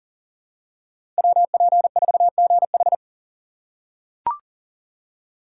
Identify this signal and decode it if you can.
Morse code, heard as a keyed tone, about 37 wpm, 700 Hz. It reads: WP4GH
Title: USA Call Signs - 5 Characters - 3x Character Spacing - 40wpm